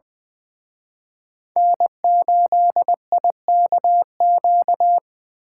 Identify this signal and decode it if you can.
Morse code, heard as a keyed tone, about 23 wpm, 700 Hz. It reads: N8IKQ
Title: Speed Racing - USA 1x3 Call Signs 20wpm